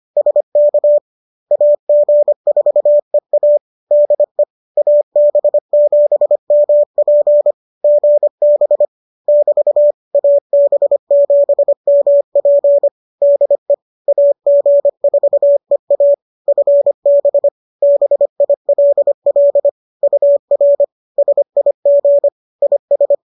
Morse code, heard as a keyed tone, about 25 wpm, 570 Hz. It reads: SK AG4EA DE AB7MP GB <BT> AB7MP DE AG4EA FB BILL UR SIG IS